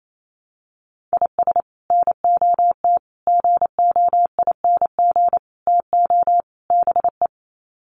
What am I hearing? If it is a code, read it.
Morse code, heard as a keyed tone, about 28 wpm, 700 Hz. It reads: IS NOT GOING TO BE
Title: Sets of 5 Words 28wpm